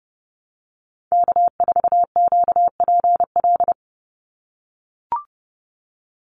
Morse code, heard as a keyed tone, about 30 wpm, 700 Hz. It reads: K4QPL
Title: Call Signs - 30wpm